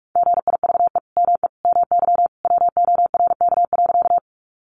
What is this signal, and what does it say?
Morse code, heard as a keyed tone, about 44 wpm, 700 Hz. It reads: GIVE ME MY WORK.
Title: Sentences from Top 200 Words 45wpm